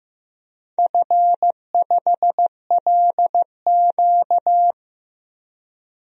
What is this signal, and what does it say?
Morse code, heard as a keyed tone, about 15 wpm, 700 Hz. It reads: F5LQ
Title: DXCC-100 Call Signs - 4 Characters - Repeated 3x - 15wpm